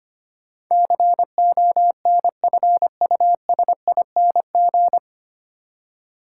Morse code, heard as a keyed tone, about 25 wpm, 700 Hz. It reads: CONFUSING
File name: Nine Letter Words - 25wpm